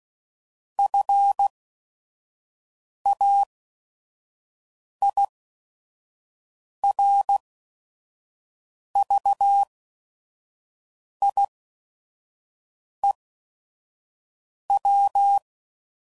Morse code, heard as a keyed tone, about 16 wpm, spaced as for 5 wpm, 800 Hz. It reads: FAIRVIEW
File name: CW Code Groups @5WPM / fromSM's / Cities